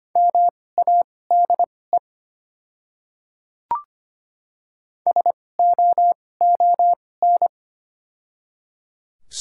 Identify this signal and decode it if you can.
Morse code, heard as a keyed tone, about 25 wpm, spaced as for 18 wpm, 700 Hz. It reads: MADE SOON
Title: 2x Character Spacing - Top 500 Words - 25wpm